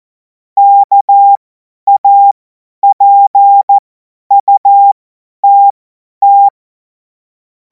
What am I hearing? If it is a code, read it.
Morse code, heard as a keyed tone, about 14 wpm, spaced as for 10 wpm, 800 Hz. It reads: KAPUTT
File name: Morsekurs / Witze / Witz 23. Juni 2026 50 F70